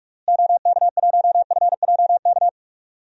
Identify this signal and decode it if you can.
Morse code, heard as a keyed tone, about 46 wpm, 700 Hz. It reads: OK1FJK
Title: DXCC-100 Call Signs - 6 Characters - Repeated 3x - 45wpm